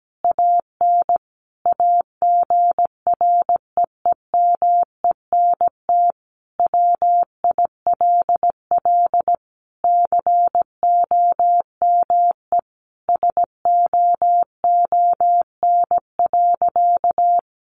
Morse code, heard as a keyed tone, about 17 wpm, 700 Hz. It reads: AN AGREEMENT WILL COME SOON.
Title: Sentences from Top 1500 Words 17wpm